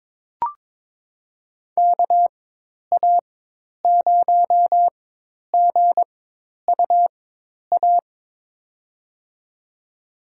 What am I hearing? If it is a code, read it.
Morse code, heard as a keyed tone, about 22 wpm, spaced as for 10 wpm, 700 Hz. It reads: KA0GUA
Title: USA Call Signs - 6 Characters - 4x Character Spacing - 22wpm